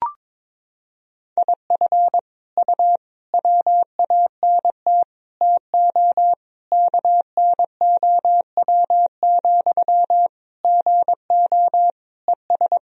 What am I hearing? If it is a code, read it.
Morse code, heard as a keyed tone, about 21 wpm, 700 Hz. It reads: IF U WANT TO KNOW, GO ES
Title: Sentences from Top 200 Words and Abbreviations 22wpm